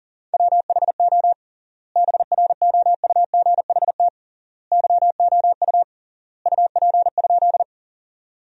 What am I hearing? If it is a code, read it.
Morse code, heard as a keyed tone, about 40 wpm, 700 Hz. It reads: WHO BROUGHT YOU UP?
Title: 3x Word-Spacing - Sentences from Top 500 Words - 40wpm